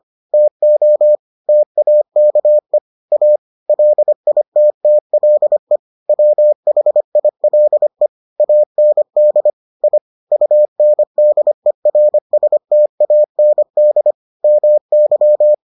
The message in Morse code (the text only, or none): TO TAKE A LITTLE WHILE AND I UNDERSTAND MY